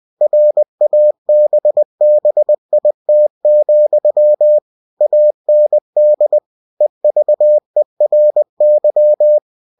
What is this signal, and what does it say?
Morse code, heard as a keyed tone, about 20 wpm, 600 Hz. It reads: RABBIT, AND EVERY